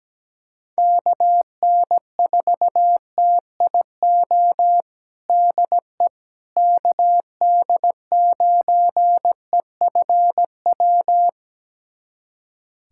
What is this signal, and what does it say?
Morse code, heard as a keyed tone, about 17 wpm, 700 Hz. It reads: KN4TIO DE KD9EFW
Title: USA Callsign DE Callsign - 6 Characters - 17wpm